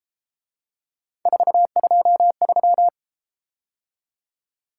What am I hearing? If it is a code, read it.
Morse code, heard as a keyed tone, about 32 wpm, 700 Hz. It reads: 423